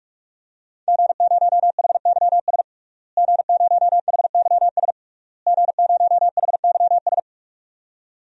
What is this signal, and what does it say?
Morse code, heard as a keyed tone, about 46 wpm, 700 Hz. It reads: G0HYS G0HYS G0HYS
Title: DXCC-100 Call Signs - 5 Characters - Repeated 3x - 45wpm